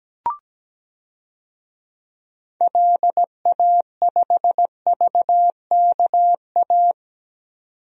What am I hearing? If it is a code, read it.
Morse code, heard as a keyed tone, about 17 wpm, 700 Hz. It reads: LA5VKA
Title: DXCC-100 Call Signs - 6 Characters - 17wpm